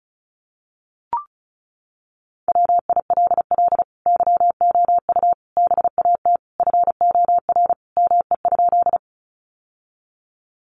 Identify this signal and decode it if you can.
Morse code, heard as a keyed tone, about 34 wpm, 700 Hz. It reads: WILL YOU BAT FOR ME?
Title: Sentences from Top 1500 Words 35wpm